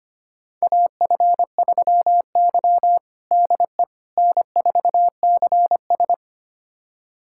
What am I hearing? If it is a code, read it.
Morse code, heard as a keyed tone, about 25 wpm, 700 Hz. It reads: AF3Y DE N4CS
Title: USA Callsign DE Callsign - 4 Characters - 25wpm